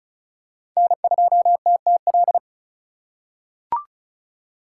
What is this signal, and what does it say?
Morse code, heard as a keyed tone, about 34 wpm, 700 Hz. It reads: N2TTL
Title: USA Call Signs - 5 Characters - 4x Character Spacing - 35wpm